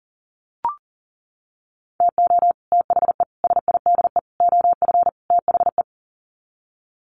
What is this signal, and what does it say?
Morse code, heard as a keyed tone, about 40 wpm, 700 Hz. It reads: TO THE SIDE OF THE
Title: Sets of 5 Words 40wpm